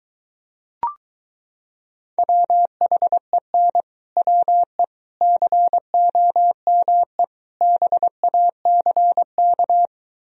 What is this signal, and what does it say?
Morse code, heard as a keyed tone, about 23 wpm, 700 Hz. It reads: WHEN WE COME BACK